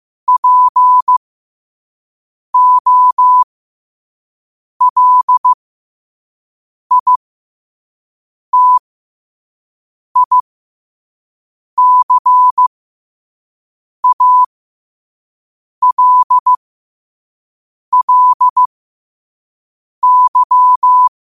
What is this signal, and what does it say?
Morse code, heard as a keyed tone, about 15 wpm, spaced as for 5 wpm, 1000 Hz. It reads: POLITICALLY
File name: qotdfiles / 5wpm